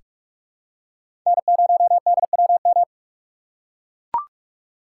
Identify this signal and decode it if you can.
Morse code, heard as a keyed tone, about 41 wpm, 700 Hz. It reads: N0DWM
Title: USA Call Signs - 5 Characters - 3x Character Spacing - 45wpm